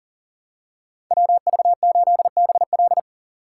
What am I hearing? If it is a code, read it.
Morse code, heard as a keyed tone, about 40 wpm, 700 Hz. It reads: WV8BL
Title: USA Call Signs - 4x Character Spacing - 40wpm